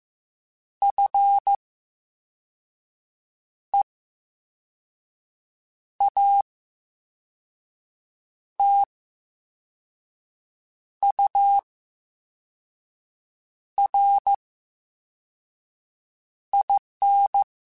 Morse code, heard as a keyed tone, about 15 wpm, spaced as for 4 wpm, 800 Hz. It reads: FEATURIN
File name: Special INGs / Special INGs-mp3s / Slow ing @13